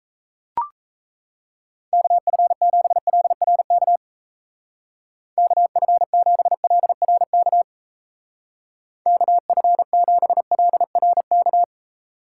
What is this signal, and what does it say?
Morse code, heard as a keyed tone, about 35 wpm, 700 Hz. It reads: KF7LRK KF7LRK KF7LRK
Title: Speed Racing - USA 2x3 Call Signs 28wpm